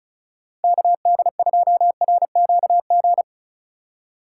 Morse code, heard as a keyed tone, about 44 wpm, 700 Hz. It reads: KD2RQG